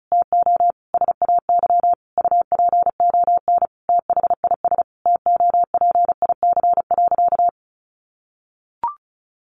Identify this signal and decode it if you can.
Morse code, heard as a keyed tone, about 34 wpm, 700 Hz. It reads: TO SAY UPON THIS TOPIC.
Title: Sentences from Top 1200 Words 35wpm